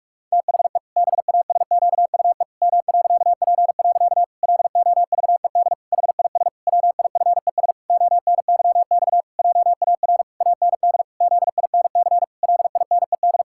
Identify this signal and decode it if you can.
Morse code, heard as a keyed tone, about 46 wpm, 700 Hz. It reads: THE BASQUE M.P. LOVED HIS WIFES ONYX JAR AND ZINC LINED
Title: Pangram Challenge v2 - 45wpm